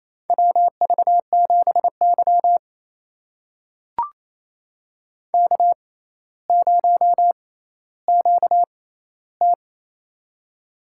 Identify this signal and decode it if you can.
Morse code, heard as a keyed tone, about 28 wpm, spaced as for 14 wpm, 700 Hz. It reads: WV7Y K0QT
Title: USA Call Signs - 4 Characters - 6x Character Spacing - 28wpm